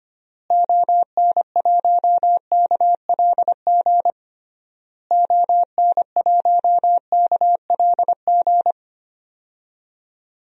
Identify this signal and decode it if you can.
Morse code, heard as a keyed tone, about 25 wpm, 700 Hz. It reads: ON1KLG ON1KLG
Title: DXCC-100 Call Signs - 6 Characters - Repeated 3x - 25wpm